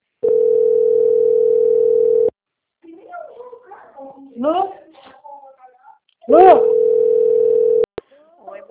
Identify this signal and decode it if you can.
Ringback tone (North America), 440 + 480 Hz, with speech.
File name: out- - - - - .52694